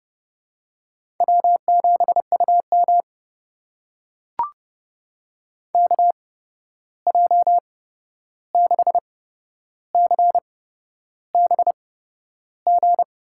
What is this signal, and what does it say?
Morse code, heard as a keyed tone, about 30 wpm, spaced as for 11 wpm, 700 Hz. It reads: W7UM KJ6CBG
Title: USA Call Signs - 8x Character Spacing - 30wpm